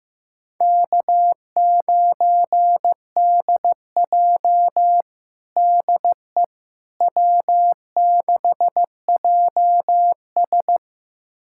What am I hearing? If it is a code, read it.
Morse code, heard as a keyed tone, about 15 wpm, 700 Hz. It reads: K9DJ DE W6JS
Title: USA Callsign DE Callsign - 4 Characters - 15wpm